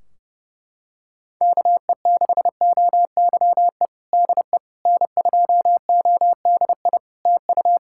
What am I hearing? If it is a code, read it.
Morse code, heard as a keyed tone, about 30 wpm, 700 Hz. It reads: KE6OYE DE N2ODI TU
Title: K1USN SST QSO Practice 30wpm